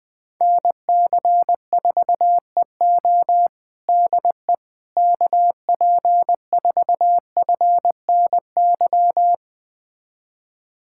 Morse code, heard as a keyed tone, about 20 wpm, 700 Hz. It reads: NC4EO DE KP4FNY
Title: USA Callsign DE Callsign 20wpm